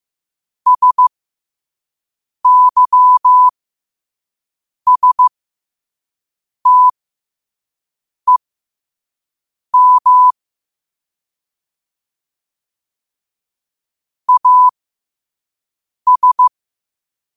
Morse code, heard as a keyed tone, about 15 wpm, spaced as for 5 wpm, 1000 Hz. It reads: SYSTEM AS
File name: qotdfiles / 5wpm